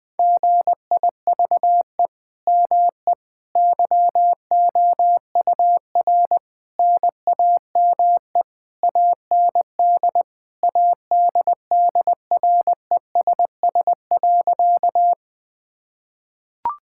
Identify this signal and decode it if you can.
Morse code, heard as a keyed tone, about 20 wpm, 700 Hz. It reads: GIVE ME YOUR NAME AND ADDRESS.
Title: Sentences from Top 1000 Words - Review Part 6 - 20wpm